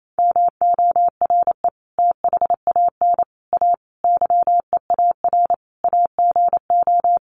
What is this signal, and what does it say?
Morse code, heard as a keyed tone, about 28 wpm, 700 Hz. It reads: MORE THAN A YEAR AGO
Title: Sets of 5 Words Encore2 28wpm